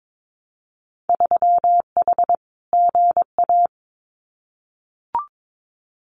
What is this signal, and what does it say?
Morse code, heard as a keyed tone, about 22 wpm, 700 Hz. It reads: 3H GA
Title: Winter Field Day Exchanges Encore 22wpm